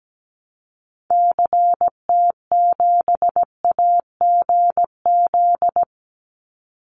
Morse code, heard as a keyed tone, about 17 wpm, 700 Hz. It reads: CT7AGZ
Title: Call Signs Repeated 2x - 17wpm